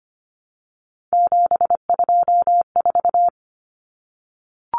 Morse code, heard as a keyed tone, about 25 wpm, 700 Hz. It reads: 724
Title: Numbers - 3-Digit 25wpm